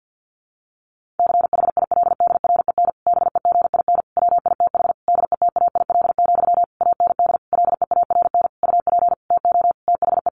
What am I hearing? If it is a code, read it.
Morse code, heard as a keyed tone, about 50 wpm, 700 Hz. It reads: CHILDREN BEGIN WITH DETAIL, AND LEARN UP TO THE
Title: Sentences from Top 1500 Words 50wpm